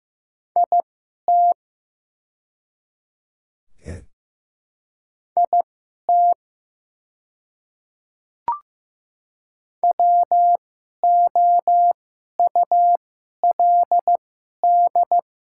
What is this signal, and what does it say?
Morse code, heard as a keyed tone, about 15 wpm, spaced as for 11 wpm, 700 Hz. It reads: IT IT WOULD
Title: 2x Character Spacing - Top 100 Words - 15wpm